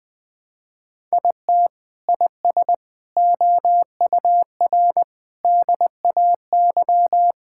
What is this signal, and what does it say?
Morse code, heard as a keyed tone, about 20 wpm, 700 Hz. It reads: IT IS OUR DAY
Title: Sentences from Top 1000 Words - Review Part 5 - 20wpm